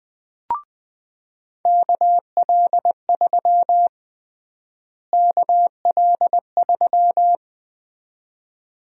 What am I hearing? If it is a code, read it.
Morse code, heard as a keyed tone, about 20 wpm, 700 Hz. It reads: KL3 KL3
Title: USA Call Sign Prefixes - Repeated 2x - 20wpm